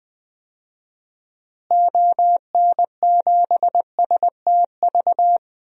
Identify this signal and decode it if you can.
Morse code, heard as a keyed tone, about 20 wpm, 700 Hz. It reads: ON7STV
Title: DXCC-100 Call Signs - 6 Characters - Repeated 2x - 20wpm